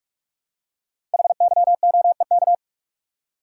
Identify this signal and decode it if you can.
Morse code, heard as a keyed tone, about 44 wpm, 700 Hz. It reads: HYOEK
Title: Code Groups - 5 Letters - 45wpm